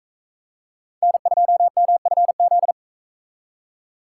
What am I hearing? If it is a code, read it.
Morse code, heard as a keyed tone, about 41 wpm, 700 Hz. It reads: N2MFZ